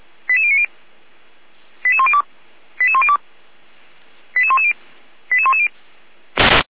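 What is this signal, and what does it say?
Five-tone selective call (ZVEI), heard as quick runs of five tones, tones of about 70 ms. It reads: 88090 88102 88102 88110 88110
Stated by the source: Five-tone ZVEI selective call received on SDR, from the Feuerwehr Orlamünde audio server.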